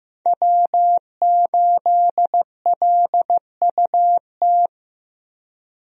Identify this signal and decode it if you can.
Morse code, heard as a keyed tone, about 15 wpm, 700 Hz. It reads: W8LUT
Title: USA Call Signs - 5 Characters - Repeated 3x - 15wpm